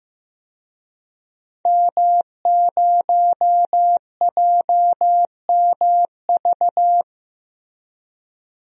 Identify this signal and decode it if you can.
Morse code, heard as a keyed tone, about 15 wpm, 700 Hz. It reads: M0JMV
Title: DXCC-100 Call Signs - 5 Characters - Repeated 3x - 15wpm